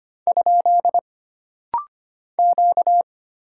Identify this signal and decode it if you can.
Morse code, heard as a keyed tone, about 25 wpm, 700 Hz. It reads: ? Q
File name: Single Character (Letter:Number:Punctuation:Prosign) - Rapid-Fire - 25wpm